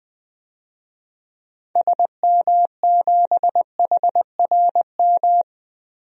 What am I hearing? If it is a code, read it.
Morse code, heard as a keyed tone, about 20 wpm, 700 Hz. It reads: SM7HRM
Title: DXCC-100 Call Signs - 6 Characters - 20wpm